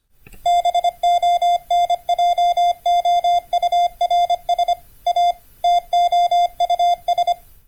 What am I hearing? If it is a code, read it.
Morse code, heard as a keyed tone, about 25 wpm, 700 Hz. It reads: BONJOURS A TOUS